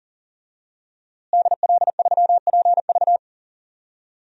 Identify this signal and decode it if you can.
Morse code, heard as a keyed tone, about 40 wpm, 700 Hz. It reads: DL3PV
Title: DXCC-100 Call Signs - 5 Characters - Repeated 2x - 40wpm